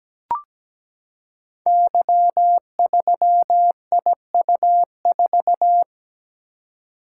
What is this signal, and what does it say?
Morse code, heard as a keyed tone, about 17 wpm, 700 Hz. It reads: Y3IU4
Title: Code Groups - 5 Letter:Numbers - 17wpm